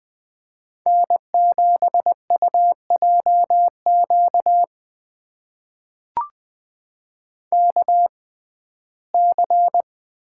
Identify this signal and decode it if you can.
Morse code, heard as a keyed tone, about 20 wpm, spaced as for 14 wpm, 700 Hz. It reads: N7UJQ KC
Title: USA Call Signs - 5 Characters - 6x Character Spacing - 20wpm